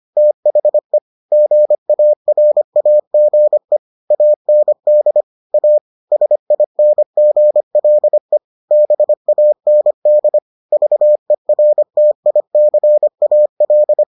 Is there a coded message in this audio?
Yes: Morse code from 0.2 to 14.0 s, heard as a keyed tone, about 25 wpm, 600 Hz. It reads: THE GARAGE AND A SINGLE BAND VERTICAL